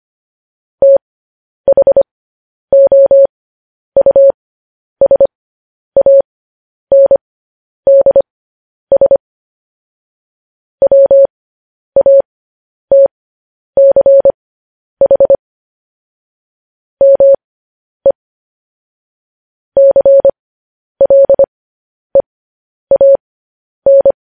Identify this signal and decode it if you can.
Morse code, heard as a keyed tone, about 25 wpm, spaced as for 10 wpm, 550 Hz. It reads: THOUSANDS WATCH ME CLEAN
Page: news headlines in morse code at 10 wpm